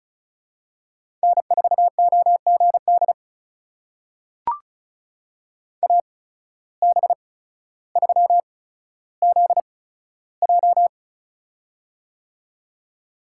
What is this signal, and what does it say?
Morse code, heard as a keyed tone, about 34 wpm, spaced as for 15 wpm, 700 Hz. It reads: N4OGD AB3ZJ
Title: USA Call Signs - 8x Character Spacing - 35wpm